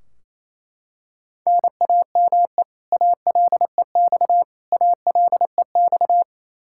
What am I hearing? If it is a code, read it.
Morse code, heard as a keyed tone, about 28 wpm, 700 Hz. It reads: NAME ALEX ALEX
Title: Practice QSO - Exchange 1 - Beginning - 28wpm